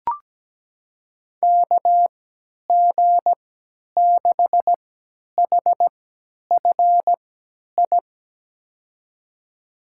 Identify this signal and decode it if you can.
Morse code, heard as a keyed tone, about 17 wpm, spaced as for 10 wpm, 700 Hz. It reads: KG6HFI
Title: USA Call Signs - 6 Characters - 3x Character Spacing - 17wpm